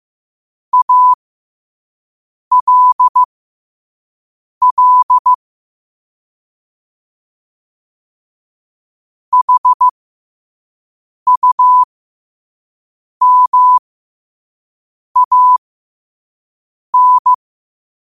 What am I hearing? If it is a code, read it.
Morse code, heard as a keyed tone, about 15 wpm, spaced as for 5 wpm, 1000 Hz. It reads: ALL HUMAN
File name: qotdfiles / 5wpm